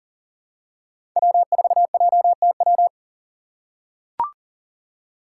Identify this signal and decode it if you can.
Morse code, heard as a keyed tone, about 39 wpm, 700 Hz. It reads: W4JTW
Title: USA Call Signs - 2x Character Spacing - 40wpm